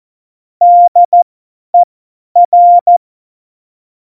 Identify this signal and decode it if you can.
Morse code, heard as a keyed tone, about 14 wpm, spaced as for 10 wpm, 700 Hz. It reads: DER
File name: Morsekurs / Witze / Witz 20. Oktober 2020, 50 F70